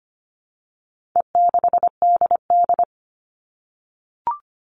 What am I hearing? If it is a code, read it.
Morse code, heard as a keyed tone, about 25 wpm, 700 Hz. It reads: E6DD
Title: All Cut Numbers - 4-Digit 25wpm